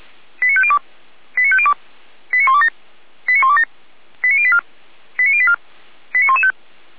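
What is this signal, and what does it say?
Five-tone selective call (ZVEI), heard as quick runs of five tones, tones of about 70 ms. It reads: 89502 89502 89127 89127 89084 89084 89115
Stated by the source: ZVEI five-tone selective call received by SDR.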